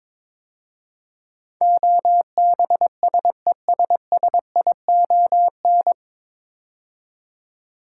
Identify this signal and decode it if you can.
Morse code, heard as a keyed tone, about 22 wpm, 700 Hz. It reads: OBSESSION
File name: 9 Letter Words - First 3 Letters Given - 22wpm